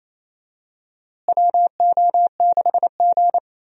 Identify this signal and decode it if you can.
Morse code, heard as a keyed tone, about 27 wpm, 700 Hz. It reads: WO6G
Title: USA Call Signs - 4 Characters - Repeated 3x - 28wpm